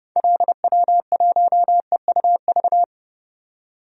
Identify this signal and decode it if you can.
Morse code, heard as a keyed tone, about 30 wpm, 700 Hz. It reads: LW1EUV
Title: DXCC-100 Call Signs - 6 Characters - 30wpm